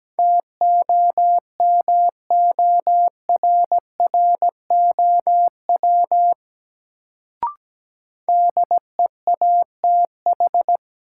Morse code, heard as a keyed tone, about 17 wpm, 700 Hz. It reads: TOMORROW DEATH
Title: Sentences from Top 1100 Words - New Word Drill - 17wpm